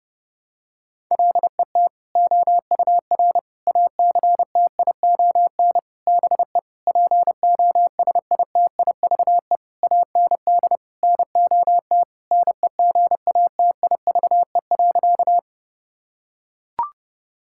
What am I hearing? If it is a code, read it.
Morse code, heard as a keyed tone, about 30 wpm, 700 Hz. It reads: LET OUR ACTION BE POSITIVE AND NOT NEGATIVE.
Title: Sentences from Top 1000 Words - Review Part 6 - 30wpm